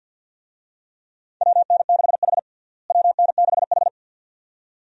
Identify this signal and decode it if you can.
Morse code, heard as a keyed tone, about 47 wpm, 700 Hz. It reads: WN6H WN6H
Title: USA Call Signs - 4 Characters - Repeated 2x - 50wpm